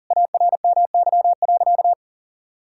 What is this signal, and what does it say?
Morse code, heard as a keyed tone, about 38 wpm, 700 Hz. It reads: ARMY.
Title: Sentences from Top 1000 Words - Review Part 6 - 40wpm